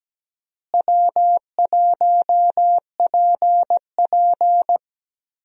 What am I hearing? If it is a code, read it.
Morse code, heard as a keyed tone, about 17 wpm, 700 Hz. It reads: W1PP
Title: USA Call Signs - 4 Characters - Repeated 2x - 17wpm